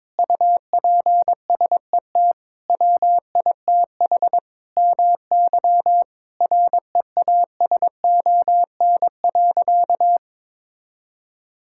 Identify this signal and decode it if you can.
Morse code, heard as a keyed tone, about 22 wpm, 700 Hz. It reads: UPSET WITH MY REASON.
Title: Sentences from Top 1000 Words - Review Part 5 - 22wpm